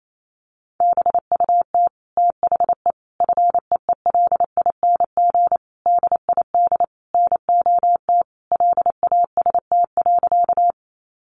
Morse code, heard as a keyed tone, about 28 wpm, 700 Hz. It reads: BUT THE FEELING DID NOT LAST.
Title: Sentences from Top 1200 Words 28wpm